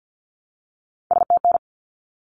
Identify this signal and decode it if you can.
Morse code, heard as a keyed tone, about 48 wpm, 700 Hz. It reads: STN
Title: Abbreviations 50wpm